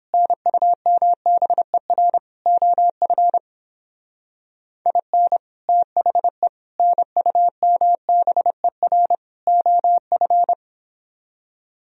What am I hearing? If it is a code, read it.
Morse code, heard as a keyed tone, about 29 wpm, 700 Hz. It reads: NUMBER OF IN THE NUMBER OF